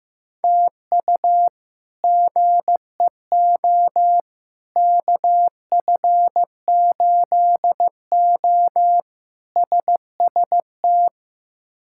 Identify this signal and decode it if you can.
Morse code, heard as a keyed tone, about 15 wpm, 700 Hz. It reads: TU GEO KF8O SST
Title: K1USN SST QSO Practice - 15wpm